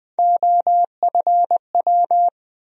Morse code, heard as a keyed tone, about 20 wpm, 700 Hz. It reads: OFW
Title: USA Call Sign Postfixes - 3 Letters 20wpm